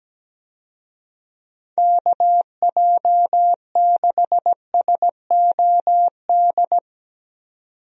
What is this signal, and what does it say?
Morse code, heard as a keyed tone, about 17 wpm, 700 Hz. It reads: KJ6SOD
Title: USA Call Signs - 6 Characters - 17wpm